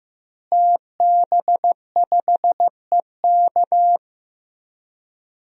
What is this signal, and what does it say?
Morse code, heard as a keyed tone, about 15 wpm, 700 Hz. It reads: TB5EK
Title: DXCC-100 Call Signs - 5 Characters - Repeated 3x - 15wpm